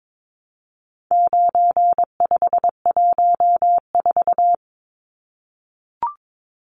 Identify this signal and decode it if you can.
Morse code, heard as a keyed tone, about 22 wpm, 700 Hz. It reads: 9514
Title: Numbers - 4-Digit 22wpm